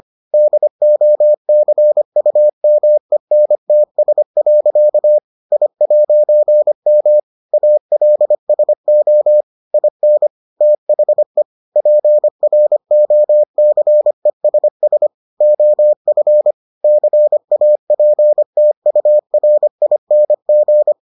Morse code, heard as a keyed tone, about 25 wpm, 600 Hz. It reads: DOCUMENTS. I'M ALSO IN THE PROCESS OF CAPTURING